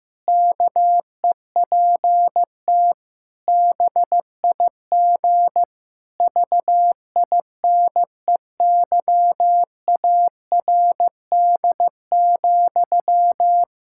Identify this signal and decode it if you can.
Morse code, heard as a keyed tone, about 15 wpm, 700 Hz. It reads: KEPT BIG VINEYARD,
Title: Pangram Challenge v2 - 15wpm